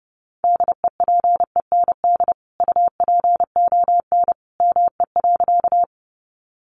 Morse code, heard as a keyed tone, about 30 wpm, 700 Hz. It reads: DEPEND UPON ME.
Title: Sentences from Top 1300 Words 30wpm